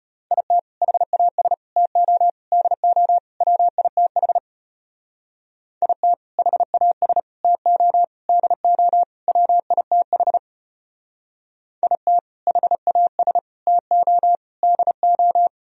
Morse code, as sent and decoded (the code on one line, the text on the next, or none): .. - / .... .- ... / - --- / -.. --- / .-- .. - .... / .. - / .... .- ... / - --- / -.. --- / .-- .. - .... / .. - / .... .- ... / - --- / -.. ---
IT HAS TO DO WITH IT HAS TO DO WITH IT HAS TO DO